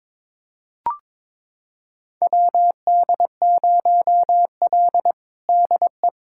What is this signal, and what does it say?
Morse code, heard as a keyed tone, about 21 wpm, 700 Hz. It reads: WD0L DE
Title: USA Callsign DE Callsign - 4 Characters - 22wpm